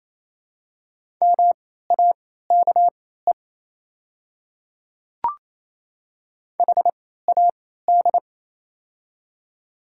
Morse code, heard as a keyed tone, about 28 wpm, spaced as for 16 wpm, 690 Hz. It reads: MAKE HAD
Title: 3x Character Spacing - Top 100 Words - 28wpm